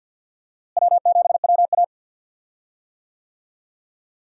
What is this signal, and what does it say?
Morse code, heard as a keyed tone, about 52 wpm, 700 Hz. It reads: W7WA
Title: USA Call Signs - 4 Characters - Repeated 3x - 50wpm